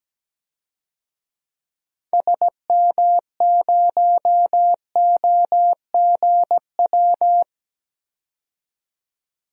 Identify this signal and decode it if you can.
Morse code, heard as a keyed tone, about 17 wpm, 700 Hz. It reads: SM0OGW
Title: DXCC-100 Call Signs - 6 Characters - 17wpm